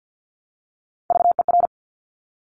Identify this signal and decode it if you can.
Morse code, heard as a keyed tone, about 50 wpm, 700 Hz. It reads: VER
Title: Top Three Letter Combinations in Words 50wpm